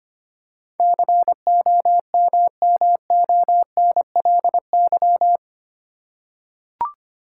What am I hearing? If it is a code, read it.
Morse code, heard as a keyed tone, about 25 wpm, 700 Hz. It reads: COMMONLY
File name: Eight Letter Words - 25wpm